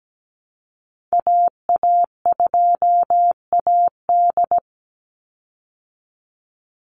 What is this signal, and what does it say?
Morse code, heard as a keyed tone, about 17 wpm, 700 Hz. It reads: AA2AD
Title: Call Signs Repeated 2x - 17wpm